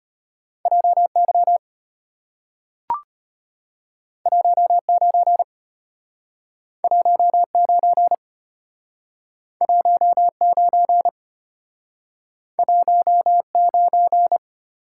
Morse code, heard as a keyed tone, about 34 wpm, 700 Hz. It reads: JY 19 19 19 19